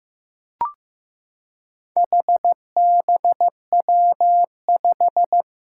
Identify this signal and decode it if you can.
Morse code, heard as a keyed tone, about 15 wpm, 700 Hz. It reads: HBW5
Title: Four Challenging Letter-Number Combinations 15wpm